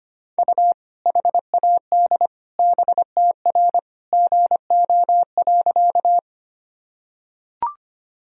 Morse code, heard as a keyed tone, about 25 wpm, 700 Hz. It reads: U HAD BTR GO.
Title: Sentences from Top 400 Words and Abbreviations 25wpm